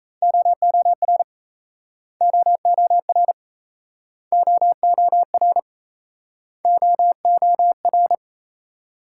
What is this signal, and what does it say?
Morse code, heard as a keyed tone, about 37 wpm, 700 Hz. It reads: OOR OOR OOR OOR